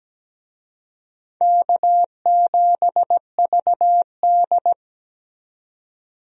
Morse code, heard as a keyed tone, about 17 wpm, 700 Hz. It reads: K7VD
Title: USA Call Signs - 4 Characters - 17wpm